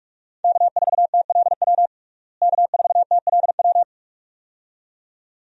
Morse code, heard as a keyed tone, about 46 wpm, 700 Hz. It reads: K4TLW K4TLW
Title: USA Call Signs - 5 Characters - Repeated 2x - 45wpm